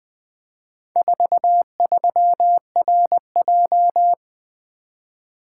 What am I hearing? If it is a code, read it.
Morse code, heard as a keyed tone, about 20 wpm, 700 Hz. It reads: 43RJ
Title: Four Letter-Number Combinations - 20wpm